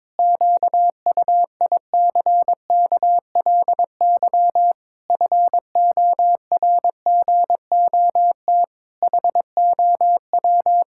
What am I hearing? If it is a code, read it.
Morse code, heard as a keyed tone, about 22 wpm, 700 Hz. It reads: QUICKLY FORGOT HOW
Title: Pangram Challenge v2 - 22wpm